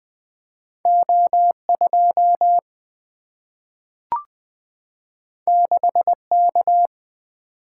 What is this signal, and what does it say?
Morse code, heard as a keyed tone, about 20 wpm, 700 Hz. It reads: O 2 6 K
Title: Two Letter-Number Combinations - 20wpm